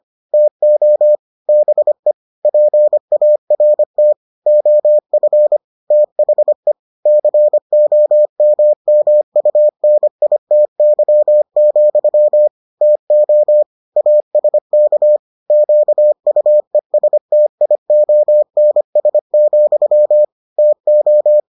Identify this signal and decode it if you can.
Morse code, heard as a keyed tone, about 25 wpm, 600 Hz. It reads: TO BE PART OF THE COMMUNITY, TO ASK QUESTIONS, TO